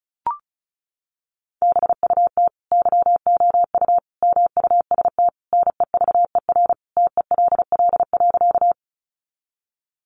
Morse code, heard as a keyed tone, about 34 wpm, 700 Hz. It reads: BUT YOU MUST NEVER TELL.
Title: Sentences from Top 300 Words 35wpm